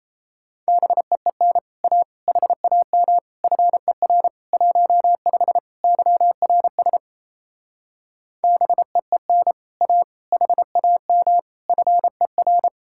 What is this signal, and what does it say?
Morse code, heard as a keyed tone, about 38 wpm, 700 Hz. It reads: BEEN A HAM FER 15 YRS BEEN A HAM FER